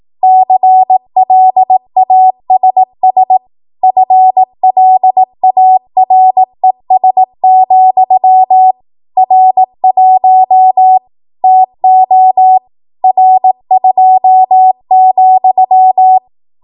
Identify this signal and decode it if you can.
Morse code, heard as a keyed tone, about 18 wpm, 750 Hz. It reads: CLASS FLARES, R1 TO R2,
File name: Morse / 18WPM